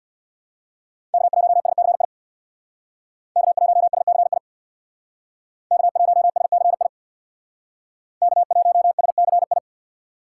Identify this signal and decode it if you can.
Morse code, heard as a keyed tone, about 62 wpm, 700 Hz. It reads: K1SCI K1SCI K1SCI K1SCI
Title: Speed Racing - USA 1x3 Call Signs 50wpm